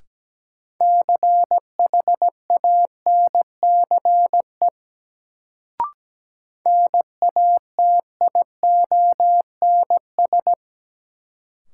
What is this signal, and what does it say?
Morse code, heard as a keyed tone, about 17 wpm, 700 Hz. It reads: CHANCE NATIONS
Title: Sets of 3 Words - New Word Drill - 17wpm